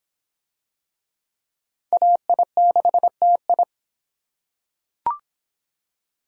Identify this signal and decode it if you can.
Morse code, heard as a keyed tone, about 23 wpm, 700 Hz. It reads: AI6TI